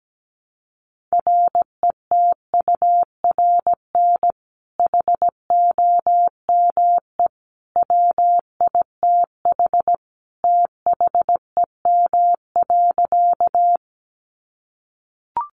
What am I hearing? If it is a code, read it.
Morse code, heard as a keyed tone, about 17 wpm, 700 Hz. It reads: RETURN HOME WITH THEM.
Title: Sentences from Top 800 Words 17wpm